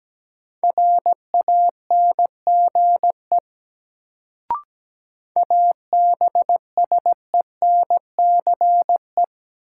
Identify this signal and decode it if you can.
Morse code, heard as a keyed tone, about 17 wpm, 700 Hz. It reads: RANGE ABSENCE
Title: Sets of 4 Words - New Word Drill - 17wpm